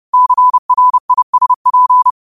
Morse code, heard as a keyed tone, about 30 wpm, 1000 Hz. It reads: CRISP
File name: qotdfiles / 30wpm